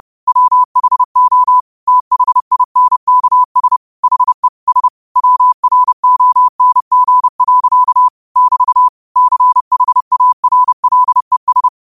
Morse code, heard as a keyed tone, about 30 wpm, 1000 Hz. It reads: WHO THINKS HE S WRONG. <BT> CHARLES